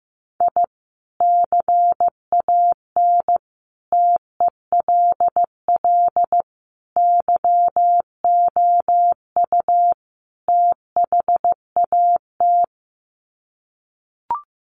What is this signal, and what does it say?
Morse code, heard as a keyed tone, about 15 wpm, 700 Hz. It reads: I CAN TELL YOU THAT
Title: Sets of 5 Words 15wpm